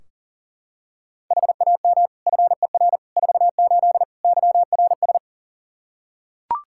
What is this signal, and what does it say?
Morse code, heard as a keyed tone, about 40 wpm, 700 Hz. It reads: HAM FER 48 YRS
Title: K1USN SST QSO Practice 40wpm